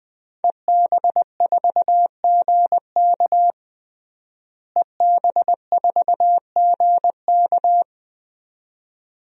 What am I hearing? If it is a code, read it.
Morse code, heard as a keyed tone, about 20 wpm, 700 Hz. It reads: EB4GK EB4GK
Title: DXCC-100 Call Signs - 5 Characters - Repeated 2x - 20wpm